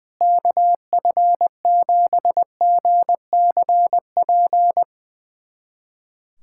Morse code, heard as a keyed tone, about 21 wpm, 700 Hz. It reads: KF7GCP
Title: USA Callsign DE Callsign - 6 Characters - 20wpm